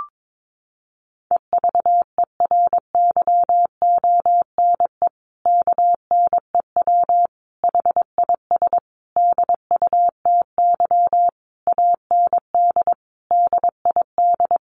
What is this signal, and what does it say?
Morse code, heard as a keyed tone, about 22 wpm, 700 Hz. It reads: EVERYONE KNEW HIS DUTY AND DID
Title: Sentences from Top 1500 Words 22wpm